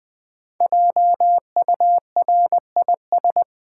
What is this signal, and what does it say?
Morse code, heard as a keyed tone, about 20 wpm, 700 Hz. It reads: JURIS
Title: Build A Word - 5 Letters At A Time - 12-Letter Words - 20wpm